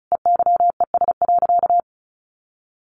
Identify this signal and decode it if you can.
Morse code, heard as a keyed tone, about 34 wpm, 700 Hz. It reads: EYES.
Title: Easy Phrases 35wpm